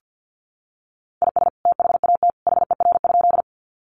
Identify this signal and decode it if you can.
Morse code, heard as a keyed tone, about 50 wpm, 700 Hz. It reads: IS THAT HER?
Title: Sentences from Top 100 Words 50wpm